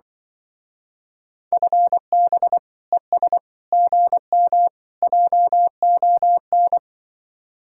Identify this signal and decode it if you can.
Morse code, heard as a keyed tone, about 24 wpm, 700 Hz. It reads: FB ES GM JON
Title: Speed-Racing Common QSO Phrases Encore - 20wpm